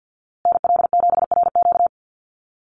Morse code, heard as a keyed tone, about 50 wpm, 700 Hz. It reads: NL7RQ
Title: USA Call Signs Repeated 3x 50wpm